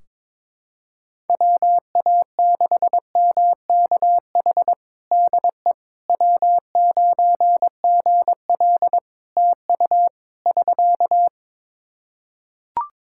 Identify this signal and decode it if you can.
Morse code, heard as a keyed tone, about 22 wpm, 700 Hz. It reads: WA6MKH DE W9GL TU <SK>
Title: Practice QSO - Exchange 4 - Ending - 22wpm